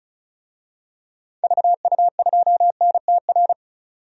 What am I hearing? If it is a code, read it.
Morse code, heard as a keyed tone, about 36 wpm, 700 Hz. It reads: VU2NTR
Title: DXCC-100 Call Signs - 6 Characters - 35wpm